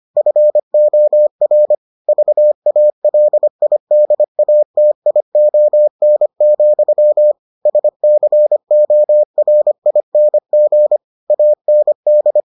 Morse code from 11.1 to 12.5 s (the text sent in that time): AND